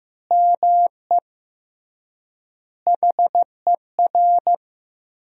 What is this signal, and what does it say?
Morse code, heard as a keyed tone, about 15 wpm, 700 Hz. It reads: ME HER
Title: 3x Word-Spacing - Sentences from Top 500 Words - 15wpm